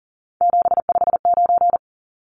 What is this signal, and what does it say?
Morse code, heard as a keyed tone, about 40 wpm, 700 Hz. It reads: 759
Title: Numbers - 3-Digit 40wpm